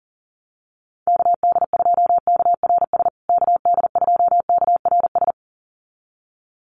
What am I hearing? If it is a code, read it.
Morse code, heard as a keyed tone, about 40 wpm, 700 Hz. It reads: KD2KRS KD2KRS
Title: USA Call Signs Repeated 2x 40wpm